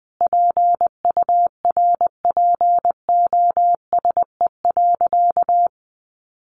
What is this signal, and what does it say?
Morse code, heard as a keyed tone, about 20 wpm, 700 Hz. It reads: PURPOSE.
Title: Sentences from Top 800 Words 20wpm